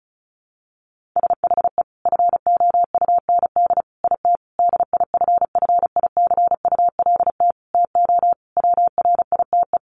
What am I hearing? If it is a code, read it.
Morse code, heard as a keyed tone, about 34 wpm, 700 Hz. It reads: SHE FOUND IT DIFFICULT TO WRITE
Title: Sentences from Top 1500 Words 35wpm